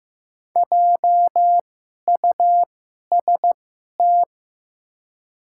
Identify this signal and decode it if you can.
Morse code, heard as a keyed tone, about 15 wpm, spaced as for 11 wpm, 700 Hz. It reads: JUST
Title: 2x Character Spacing - Top 300 Words - 15wpm